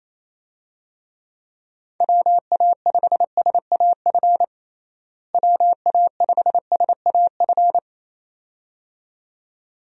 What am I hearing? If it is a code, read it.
Morse code, heard as a keyed tone, about 27 wpm, 700 Hz. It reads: WA5SAF WA5SAF
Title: USA Call Signs - 6 Characters - Repeated 2x - 28wpm